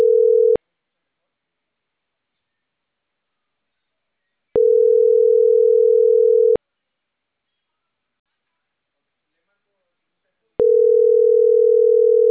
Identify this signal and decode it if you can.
Ringback tone (North America), 440 + 480 Hz, with speech.
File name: CENTRAL CAEA / out agcaea01 - .75020